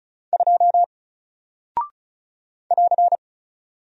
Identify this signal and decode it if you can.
Morse code, heard as a keyed tone, about 36 wpm, 700 Hz. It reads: 2 <AR>
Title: Single Character (Letter:Number:Punctuation:Prosign) Alt 35wpm